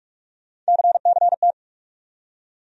Morse code, heard as a keyed tone, about 45 wpm, 700 Hz. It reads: CCT
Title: USA Call Sign Postfixes - 3 Letters 45wpm